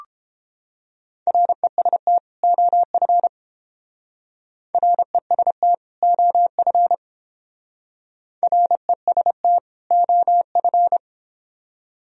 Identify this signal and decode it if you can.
Morse code, heard as a keyed tone, about 29 wpm, 700 Hz. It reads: REST OF REST OF REST OF